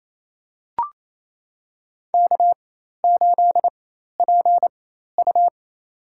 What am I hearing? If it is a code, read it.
Morse code, heard as a keyed tone, about 28 wpm, spaced as for 13 wpm, 700 Hz. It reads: K8PU
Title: USA Call Signs - 4 Characters - 4x Character Spacing - 28wpm